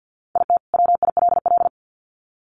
Morse code, heard as a keyed tone, about 50 wpm, 700 Hz. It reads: IT WILL
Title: Sets of 2 Words 50wpm